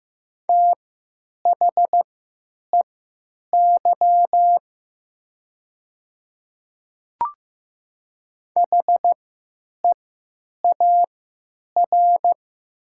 Morse code, heard as a keyed tone, about 15 wpm, spaced as for 9 wpm, 700 Hz. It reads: THEY HEAR
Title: 3x Character Spacing - Top 300 Words - 15wpm